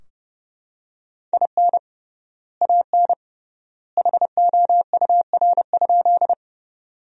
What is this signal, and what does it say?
Morse code, heard as a keyed tone, about 30 wpm, 700 Hz. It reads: IN AN HOUR?
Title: 3x Word-Spacing - Sentences from Top 400 Words - 30wpm